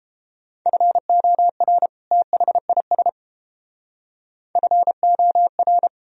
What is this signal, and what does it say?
Morse code, heard as a keyed tone, about 31 wpm, 700 Hz. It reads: FOR THIS FOR